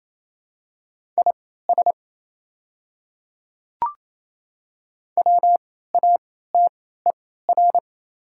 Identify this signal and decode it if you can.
Morse code, heard as a keyed tone, about 28 wpm, spaced as for 16 wpm, 700 Hz. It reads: IS WATER
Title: 3x Character Spacing - Top 100 Words - 28wpm